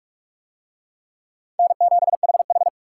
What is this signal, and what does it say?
Morse code, heard as a keyed tone, about 40 wpm, 700 Hz. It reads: N7HH